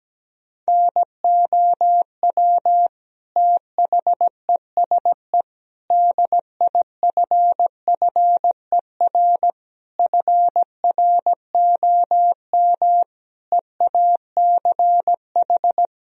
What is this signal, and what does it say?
Morse code, heard as a keyed tone, about 17 wpm, 700 Hz. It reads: NOW THESE DIFFER FROM EACH
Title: Sentences from Top 1000 Words - Review Part 1 - 17wpm